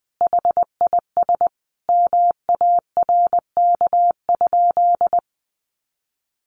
Morse code, heard as a keyed tone, about 20 wpm, 700 Hz. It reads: HIS MARK?
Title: Sentences from Top 300 Words 20wpm